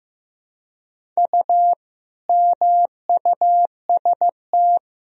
Morse code, heard as a keyed tone, about 15 wpm, 700 Hz. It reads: U MUST
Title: Sentences from Top 300 Words and Abbreviations 15wpm